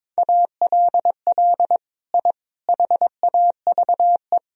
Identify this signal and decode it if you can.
Morse code, heard as a keyed tone, about 22 wpm, 700 Hz. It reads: ALL I HAVE
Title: Sentences from Top 1000 Words - Review Part 3 - 22wpm